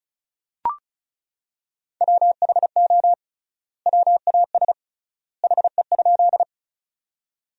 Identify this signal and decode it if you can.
Morse code, heard as a keyed tone, about 36 wpm, 700 Hz. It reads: WHO WAS HE?
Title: 3x Word-Spacing - Sentences from Top 100 Words - 35wpm